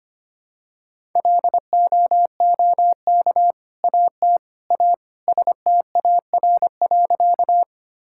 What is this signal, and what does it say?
Morse code, heard as a keyed tone, about 25 wpm, 700 Hz. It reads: LOOK AT A STAR.
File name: practice / 25wpm / Sentences with 2-Word Phrases - 25wpm